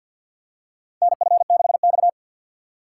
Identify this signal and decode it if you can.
Morse code, heard as a keyed tone, about 50 wpm, 700 Hz. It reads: NF6X
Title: USA Call Signs - 4 Characters - 3x Character Spacing - 50wpm